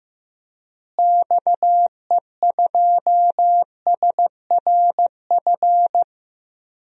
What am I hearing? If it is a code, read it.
Morse code, heard as a keyed tone, about 15 wpm, 700 Hz. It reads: XE2SRF
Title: DXCC-100 Call Signs - 6 Characters - Repeated 3x - 15wpm